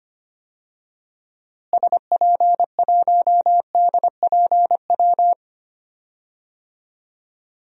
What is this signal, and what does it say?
Morse code, heard as a keyed tone, about 25 wpm, 700 Hz. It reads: SP1DPW
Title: DXCC-100 Call Signs - 6 Characters - Repeated 2x - 25wpm